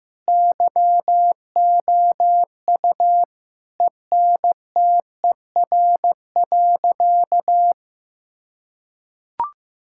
Morse code, heard as a keyed tone, about 15 wpm, 700 Hz. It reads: YOU ENTER.
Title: Sentences from Top 1600 Words 15wpm